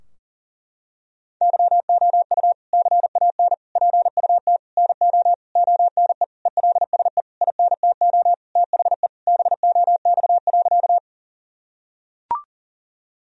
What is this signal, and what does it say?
Morse code, heard as a keyed tone, about 40 wpm, 700 Hz. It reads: YOU CAN PUT NO ONE ELSE INTO THE BOX.
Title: Sentences from Top 1000 Words - Review Part 2 - 40wpm